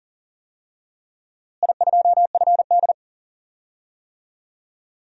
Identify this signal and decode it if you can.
Morse code, heard as a keyed tone, about 38 wpm, 700 Hz. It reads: I2FD
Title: DXCC-100 Call Signs - 4 Characters - Repeated 3x - 40wpm